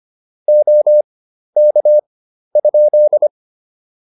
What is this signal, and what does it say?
Morse code, heard as a keyed tone, about 25 wpm, spaced as for 12 wpm, 600 Hz. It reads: OK?